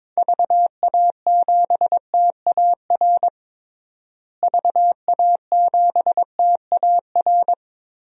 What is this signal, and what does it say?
Morse code, heard as a keyed tone, about 22 wpm, 700 Hz. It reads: VA7TAR VA7TAR
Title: DXCC-100 Call Signs - 6 Characters - Repeated 3x - 22wpm